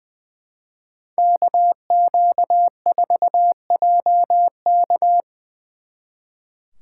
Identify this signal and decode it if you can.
Morse code, heard as a keyed tone, about 22 wpm, 700 Hz. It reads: KQ4JK
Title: Speed Racing - USA 2x2 Call Signs 20wpm